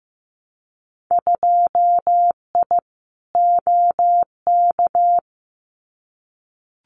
Morse code, heard as a keyed tone, about 15 wpm, 700 Hz. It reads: 2I OK
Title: Winter Field Day Exchanges Encore 15wpm